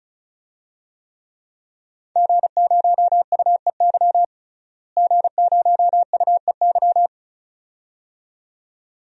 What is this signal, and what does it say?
Morse code, heard as a keyed tone, about 36 wpm, 700 Hz. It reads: G0UEY G0UEY
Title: DXCC-100 Call Signs - 5 Characters - Repeated 2x - 35wpm